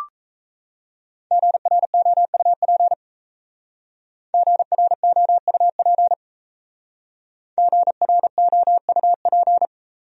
Morse code, heard as a keyed tone, about 36 wpm, 700 Hz. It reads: GROUP GROUP GROUP